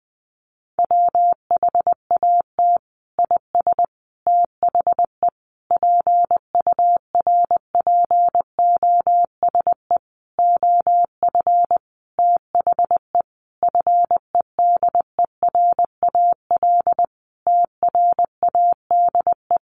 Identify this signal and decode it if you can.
Morse code, heard as a keyed tone, about 20 wpm, 700 Hz. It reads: WHAT IS THE PURPOSE OF THE FEDERAL TRADE
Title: Sentences from Top 1400 Words 20wpm